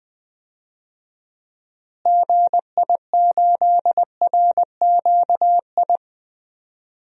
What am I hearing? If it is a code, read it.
Morse code, heard as a keyed tone, about 20 wpm, 700 Hz. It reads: GI8RQI
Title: DXCC-100 Call Signs - 6 Characters - Repeated 2x - 20wpm